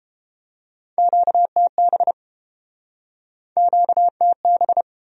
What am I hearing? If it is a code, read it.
Morse code, heard as a keyed tone, about 33 wpm, 700 Hz. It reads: QTB QTB